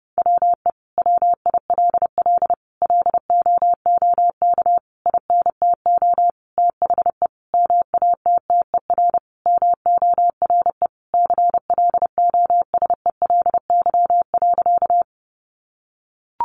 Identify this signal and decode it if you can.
Morse code, heard as a keyed tone, about 30 wpm, 700 Hz. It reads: WE WILL LOOK INTO THE MATTER MORE CLOSELY.
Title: Sentences from Top 1400 Words 30wpm